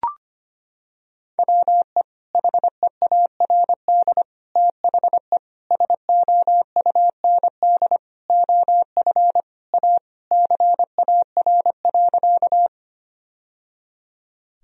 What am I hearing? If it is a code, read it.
Morse code, heard as a keyed tone, about 25 wpm, 700 Hz. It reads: WE HEARD THE SOUND OF A CAR.
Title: Sentences from Top 1000 Words - Review Part 1 - 25wpm